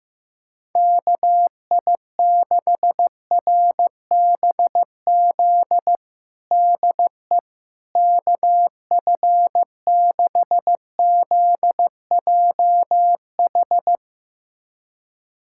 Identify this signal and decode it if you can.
Morse code, heard as a keyed tone, about 15 wpm, 700 Hz. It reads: KI6RBZ DE KF6ZJH
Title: USA Callsign DE Callsign - 6 Characters - 15wpm